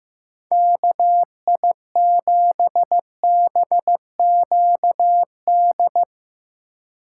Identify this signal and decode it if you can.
Morse code, heard as a keyed tone, about 19 wpm, 700 Hz. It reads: KI7BQD
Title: Speed Racing - USA 2x3 Call Signs 15wpm